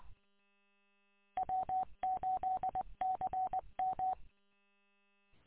Morse code, heard as a keyed tone, about 20 wpm, 750 Hz. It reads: W8CM
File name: w8cmn- -combined